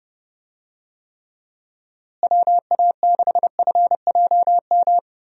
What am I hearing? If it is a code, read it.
Morse code, heard as a keyed tone, about 31 wpm, 700 Hz. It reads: WA6FJM
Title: USA Call Signs - 6 Characters - Repeated 2x - 30wpm